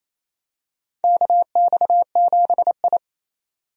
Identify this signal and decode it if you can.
Morse code, heard as a keyed tone, about 28 wpm, 700 Hz. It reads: KX7I
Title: USA Call Signs - 4 Characters - 28wpm